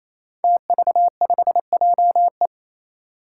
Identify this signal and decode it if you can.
Morse code, heard as a keyed tone, about 28 wpm, 700 Hz. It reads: TV5JE
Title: Five Challenging Letter-Number Combinations 28wpm